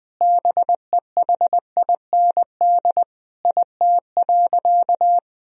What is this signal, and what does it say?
Morse code, heard as a keyed tone, about 21 wpm, 700 Hz. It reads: BEHIND IT.
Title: Sentences from Top 500 Words and Abbreviations 20wpm